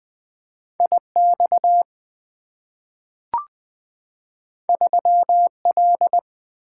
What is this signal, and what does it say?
Morse code, heard as a keyed tone, about 20 wpm, 700 Hz. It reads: I X 3 L
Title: Two Letter-Number Combinations - 20wpm